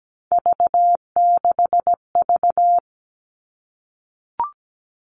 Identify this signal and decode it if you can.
Morse code, heard as a keyed tone, about 17 wpm, 700 Hz. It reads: V6V
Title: All Cut Numbers - 3-Digit 17wpm